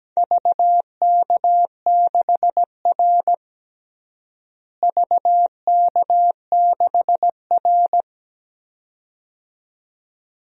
Morse code, heard as a keyed tone, about 17 wpm, 700 Hz. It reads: VK6R VK6R
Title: DXCC-100 Call Signs - 4 Characters - Repeated 2x - 17wpm